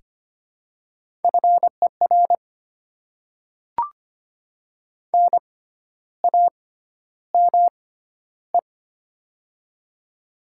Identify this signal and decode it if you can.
Morse code, heard as a keyed tone, about 25 wpm, spaced as for 12 wpm, 700 Hz. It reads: FER NAME
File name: Top 20 QSO Elements - 6x Character Spacing - 25wpm